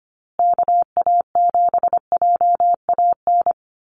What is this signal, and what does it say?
Morse code, heard as a keyed tone, about 25 wpm, 700 Hz. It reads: KA7JAN
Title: USA Call Signs Repeated 2x 25wpm